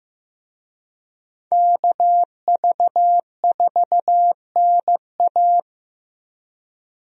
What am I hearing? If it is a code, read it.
Morse code, heard as a keyed tone, about 15 wpm, 700 Hz. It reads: KV4NA
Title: USA Call Signs - 5 Characters - Repeated 2x - 15wpm